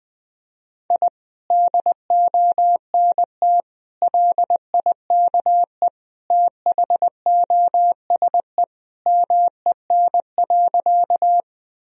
Morse code, heard as a keyed tone, about 19 wpm, 700 Hz. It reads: I DONT LIKE THOSE MEN.
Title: Sentences from Top 1000 Words - Review Part 1 - 20wpm